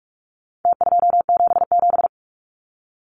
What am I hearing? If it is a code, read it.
Morse code, heard as a keyed tone, about 44 wpm, 700 Hz. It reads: T277
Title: Numbers - Contest Style - 4-Digit 45wpm